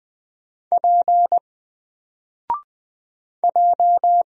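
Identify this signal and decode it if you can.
Morse code, heard as a keyed tone, about 20 wpm, 700 Hz. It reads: P J
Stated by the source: Single letter and number Morse drill at 20 wpm, from N4LCD.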